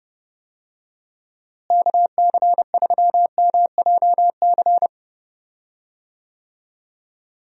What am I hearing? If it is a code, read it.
Morse code, heard as a keyed tone, about 31 wpm, 700 Hz. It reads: KC3MJC
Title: USA Call Signs - 6 Characters - Repeated 3x - 30wpm